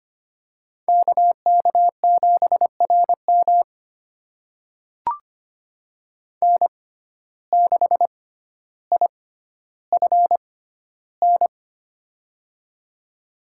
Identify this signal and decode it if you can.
Morse code, heard as a keyed tone, about 25 wpm, spaced as for 13 wpm, 690 Hz. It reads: KK7RM N6IFN
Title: USA Call Signs - 5 Characters - 6x Character Spacing - 25wpm